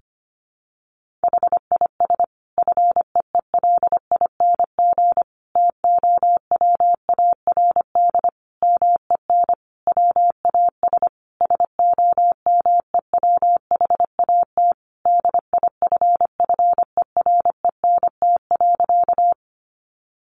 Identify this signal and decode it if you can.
Morse code, heard as a keyed tone, about 25 wpm, 700 Hz. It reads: HIS FEELING TOWARD MEN WAS SOMEWHAT DIFFERENT.
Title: Sentences from Top 1300 Words 25wpm